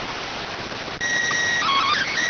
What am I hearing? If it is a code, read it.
SSTV (slow-scan television), PD 290.